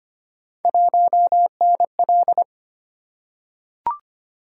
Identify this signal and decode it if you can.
Morse code, heard as a keyed tone, about 25 wpm, 700 Hz. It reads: 1NL
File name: Three Letter-Number Combinations - 25wpm